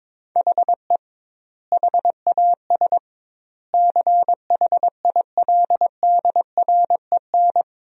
Morse code, heard as a keyed tone, about 22 wpm, 700 Hz. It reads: HE HAS CHILDREN
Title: 2x Word-Spacing - Sentences from Top 300 Words - 22wpm